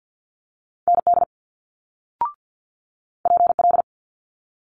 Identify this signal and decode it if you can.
Morse code, heard as a keyed tone, about 50 wpm, 700 Hz. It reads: ND PL
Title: Multi-letter Phonemes Quick Fire 50wpm